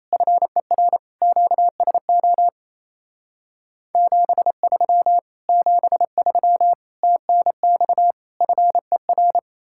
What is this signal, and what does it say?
Morse code, heard as a keyed tone, about 37 wpm, 700 Hz. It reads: FER QSO 73 73 TNX FER